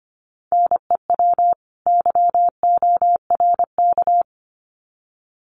Morse code, heard as a keyed tone, about 25 wpm, 700 Hz. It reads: NEW YORK
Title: Sets of 2 Words 25wpm